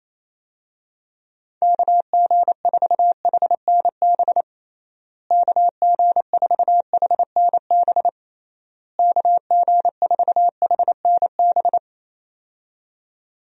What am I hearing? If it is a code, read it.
Morse code, heard as a keyed tone, about 28 wpm, 700 Hz. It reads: KG4HNB KG4HNB KG4HNB
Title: USA Call Signs - 6 Characters - Repeated 3x - 28wpm